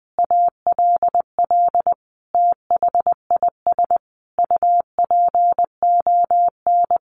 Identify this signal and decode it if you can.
Morse code, heard as a keyed tone, about 20 wpm, 700 Hz. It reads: ALL THIS UPON
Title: Sentences from Top 1400 Words 20wpm